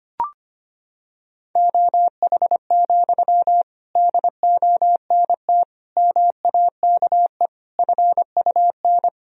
Morse code, heard as a keyed tone, about 25 wpm, 700 Hz. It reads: OH, DONT MAKE FUN
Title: Sentences from Top 1000 Words - Review Part 4 - 25wpm